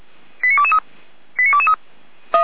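Five-tone selective call (ZVEI), heard as quick runs of five tones, tones of about 70 ms. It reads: 89223 89223
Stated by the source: ZVEI five-tone call 89220 received on SDR by Feuerwehr Orlamünde.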